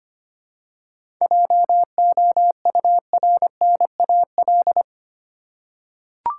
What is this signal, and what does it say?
Morse code, heard as a keyed tone, about 25 wpm, 700 Hz. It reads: JOURNAL
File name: practice / 25wpm / Difficult Words with Hints - 25wpm